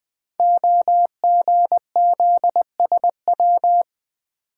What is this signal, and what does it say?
Morse code, heard as a keyed tone, about 21 wpm, 700 Hz. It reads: OGZSW
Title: Code Groups - 5 Letter:Numbers - 20wpm